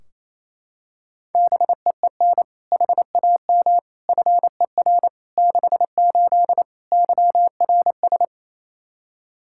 Morse code, heard as a keyed tone, about 28 wpm, 700 Hz. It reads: BEEN HAM FER 68 YRS
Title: Practice QSO - Exchange 3 - More Info - 28wpm